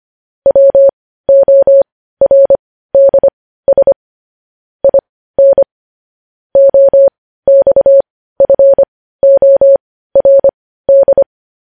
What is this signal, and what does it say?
Morse code, heard as a keyed tone, about 25 wpm, spaced as for 15 wpm, 550 Hz. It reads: WORDS IN OXFORD
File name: podcast / News headlines-1x-25x15